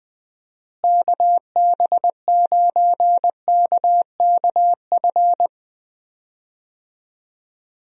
Morse code, heard as a keyed tone, about 20 wpm, 700 Hz. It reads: KB9KKF
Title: USA Call Signs - 6 Characters - Repeated 2x - 20wpm